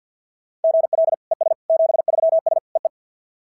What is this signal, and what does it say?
Morse code, heard as a keyed tone, about 52 wpm, 690 Hz. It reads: GL ES 73S EE
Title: SOTA QSO Practice 50wpm